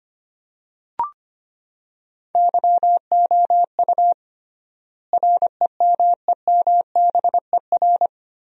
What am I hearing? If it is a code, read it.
Morse code, heard as a keyed tone, about 25 wpm, 700 Hz. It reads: YOU REMEMBER
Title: 3x Word-Spacing - Sentences from Top 400 Words - 25wpm